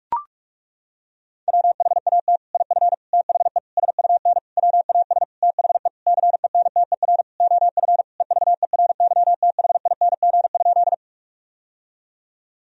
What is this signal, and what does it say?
Morse code, heard as a keyed tone, about 42 wpm, 700 Hz. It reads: WHAT IF THE SUN WAS THE CENTER OF EVERYTHING?
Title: Sentences from Top 1000 Words - Review Part 2 - 45wpm